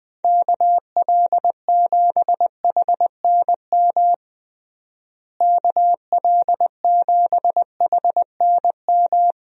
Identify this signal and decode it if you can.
Morse code, heard as a keyed tone, about 20 wpm, 700 Hz. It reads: KL7HNM KL7HNM
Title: USA Call Signs - 6 Characters - Repeated 3x - 20wpm